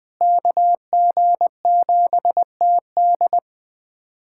Morse code, heard as a keyed tone, about 20 wpm, 700 Hz. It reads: KG7TD
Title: USA Callsign DE Callsign - 5 Characters - 20wpm